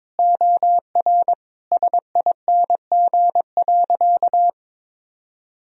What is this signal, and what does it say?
Morse code, heard as a keyed tone, about 22 wpm, 700 Hz. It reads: OR SING.
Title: Sentences from Top 1000 Words - Review Part 2 - 22wpm